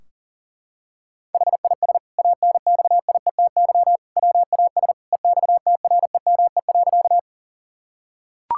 Morse code, heard as a keyed tone, about 40 wpm, 700 Hz. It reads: HIS ANXIETY WAS EXTREME.
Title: Sentences from Top 1600 Words 40wpm